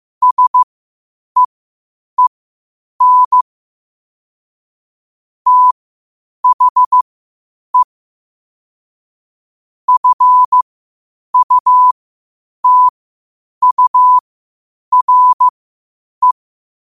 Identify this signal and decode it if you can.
Morse code, heard as a keyed tone, about 15 wpm, spaced as for 8 wpm, 1000 Hz. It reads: SEEN THE FUTURE
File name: qotdfiles / 8wpm